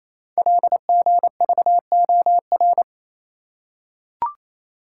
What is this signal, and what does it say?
Morse code, heard as a keyed tone, about 29 wpm, 700 Hz. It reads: LGVOR
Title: Code Groups - 5 Letters - 28wpm